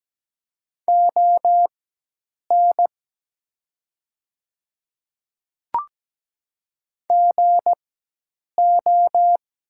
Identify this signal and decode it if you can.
Morse code, heard as a keyed tone, about 17 wpm, spaced as for 8 wpm, 700 Hz. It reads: ON GO
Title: 4x Character Spacing - Top 100 Words - 17wpm